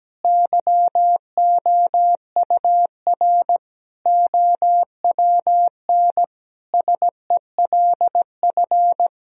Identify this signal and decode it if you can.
Morse code, heard as a keyed tone, about 17 wpm, 700 Hz. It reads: YOUR OWN SELF
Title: Sentences from Top 1000 Words - Review Part 1 - 17wpm